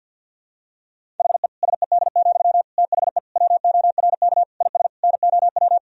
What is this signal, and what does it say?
Morse code, heard as a keyed tone, about 46 wpm, 700 Hz. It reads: HE SED, THE WORK IS NOW
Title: Sentences from Top 400 Words and Abbreviations 50wpm